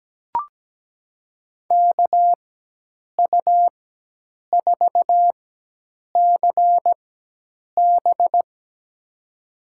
Morse code, heard as a keyed tone, about 17 wpm, spaced as for 8 wpm, 700 Hz. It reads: KU4CB
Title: USA Call Signs - 5 Characters - 4x Character Spacing - 17wpm